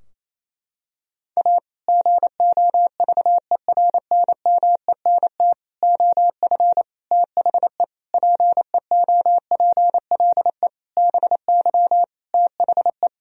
Morse code, heard as a keyed tone, about 27 wpm, 700 Hz. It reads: A GOVERNMENT OF THE PEOPLE BY THE
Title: Sentences from Top 1000 Words - Review Part 3 - 28wpm